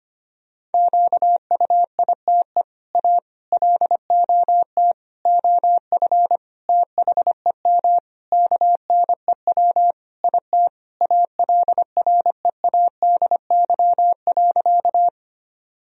Morse code, heard as a keyed tone, about 25 wpm, 700 Hz. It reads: QUITE A LOT OF THEM KNEW IT ALREADY.
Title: Sentences from Top 1000 Words - Review Part 3 - 25wpm